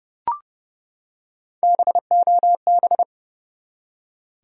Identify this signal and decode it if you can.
Morse code, heard as a keyed tone, about 32 wpm, 700 Hz. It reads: BOB
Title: USA Common Names - Max 3 Letters - 30wpm